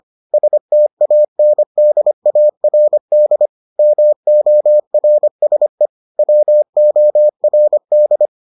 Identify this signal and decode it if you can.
Morse code, heard as a keyed tone, about 25 wpm, 600 Hz. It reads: STANDARD MORSE WORD